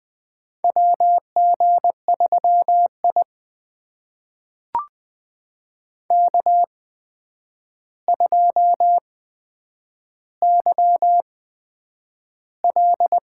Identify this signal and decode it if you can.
Morse code, heard as a keyed tone, about 20 wpm, spaced as for 9 wpm, 700 Hz. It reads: WG3I K2YL
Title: USA Call Signs - 4 Characters - 8x Character Spacing - 20wpm